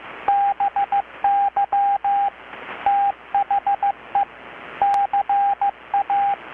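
Morse code, heard as a keyed tone, about 15 wpm, 780 Hz, with radio noise. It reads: BY THE CA
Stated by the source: Morse from KPH's Night of Nights XX maritime radiotelegraphy event, received in Chiapas, Mexico about 3,800 km away on a Kenwood R-600 with a grounded 10 m long-wire antenna.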